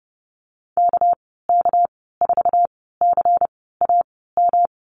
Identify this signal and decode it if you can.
Morse code, heard as a keyed tone, about 30 wpm, spaced as for 17 wpm, 700 Hz. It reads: KK4CAM
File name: Lesson 37g - Letter x - Call Signs 30wpm